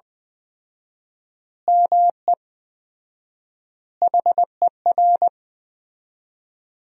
Morse code, heard as a keyed tone, about 20 wpm, 700 Hz. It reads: ME HER
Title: 4x Word-Spacing - Sentences from Top 500 Words - 20wpm